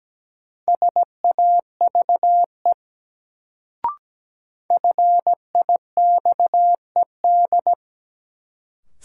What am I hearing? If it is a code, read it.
Morse code, heard as a keyed tone, about 17 wpm, 700 Hz. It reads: SAVE FIXED
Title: Sentences from Top 1100 Words - New Word Drill - 17wpm